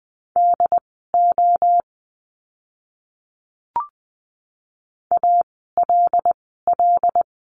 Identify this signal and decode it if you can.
Morse code, heard as a keyed tone, about 20 wpm, spaced as for 15 wpm, 700 Hz. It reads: DO ALL
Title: 101 Lesson 17f - Letter u - Words 20wpm